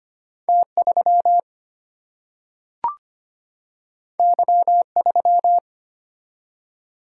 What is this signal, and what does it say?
Morse code, heard as a keyed tone, about 25 wpm, 700 Hz. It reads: T 3 Y 3
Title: Single-Letter Single-Number 25wpm